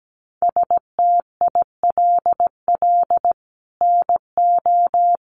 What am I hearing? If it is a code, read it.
Morse code, heard as a keyed tone, about 17 wpm, 700 Hz. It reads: STILL NO
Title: Sentences from Top 1400 Words 17wpm